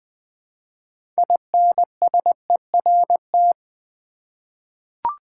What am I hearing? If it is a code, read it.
Morse code, heard as a keyed tone, about 20 wpm, 700 Hz. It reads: INSERT
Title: 6 Letter Words - First 3 Letters Given - 20wpm